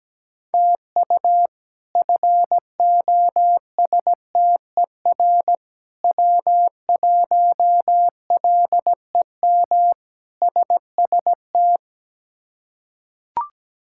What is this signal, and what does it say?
Morse code, heard as a keyed tone, about 17 wpm, 700 Hz. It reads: TU FOSTER W1LEM SST
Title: K1USN SST QSO Practice - 17wpm